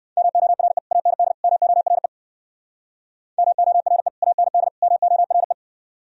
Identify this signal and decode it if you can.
Morse code, heard as a keyed tone, about 54 wpm, 700 Hz. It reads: MORE AND MORE MORE AND MORE
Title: Speed-Racing - Sets of 3 Words 45wpm